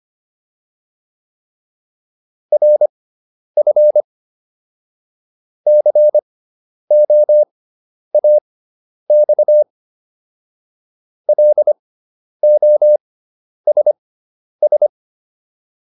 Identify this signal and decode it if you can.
Morse code, heard as a keyed tone, about 25 wpm, spaced as for 10 wpm, 600 Hz. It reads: RF COAX LOSS